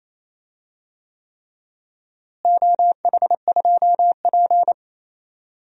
Morse code, heard as a keyed tone, about 28 wpm, 700 Hz. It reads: OH2P
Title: DXCC-100 Call Signs - 4 Characters - Repeated 3x - 28wpm